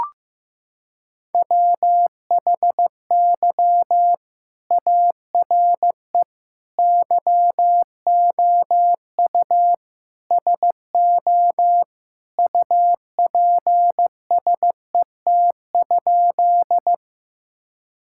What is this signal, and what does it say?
Morse code, heard as a keyed tone, about 15 wpm, 700 Hz. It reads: WHY ARE YOU SO UPSET?
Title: Sentences from Top 1000 Words - Review Part 5 - 15wpm